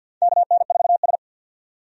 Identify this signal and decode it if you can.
Morse code, heard as a keyed tone, about 52 wpm, 700 Hz. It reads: KN4S
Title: USA Call Signs - 4 Characters - 50wpm